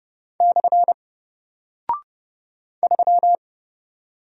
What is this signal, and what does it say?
Morse code, heard as a keyed tone, about 30 wpm, 700 Hz. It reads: / 3
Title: Single Character (Letter:Number:Punctuation:Prosign) Alt 30wpm